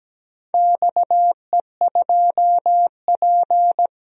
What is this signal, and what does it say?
Morse code, heard as a keyed tone, about 17 wpm, 700 Hz. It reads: XE2P
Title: DXCC-100 Call Signs - 4 Characters - 17wpm